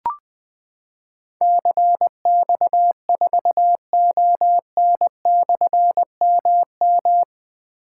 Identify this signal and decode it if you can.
Morse code, heard as a keyed tone, about 20 wpm, 700 Hz. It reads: CX4ON/MM
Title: Difficult Call Signs 20wpm